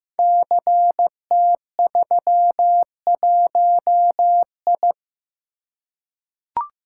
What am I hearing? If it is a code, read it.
Morse code, heard as a keyed tone, about 15 wpm, 700 Hz. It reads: CT31I
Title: Code Groups - 5 Letter:Numbers - 15wpm